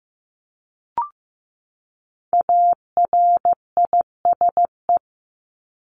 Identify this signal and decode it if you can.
Morse code, heard as a keyed tone, about 15 wpm, 700 Hz. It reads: ARISE
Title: Top 1501-1750 Words 15wpm